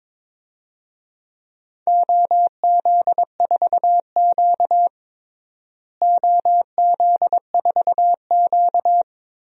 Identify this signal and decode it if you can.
Morse code, heard as a keyed tone, about 22 wpm, 700 Hz. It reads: OZ4Q OZ4Q
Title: DXCC-100 Call Signs - 4 Characters - Repeated 3x - 22wpm